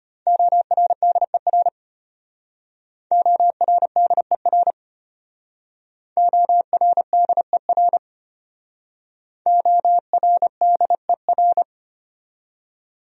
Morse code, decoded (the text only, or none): ORDER ORDER ORDER ORDER